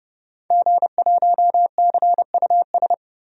Morse code, heard as a keyed tone, about 30 wpm, 700 Hz. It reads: G1CUS
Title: DXCC-100 Call Signs - 5 Characters - 30wpm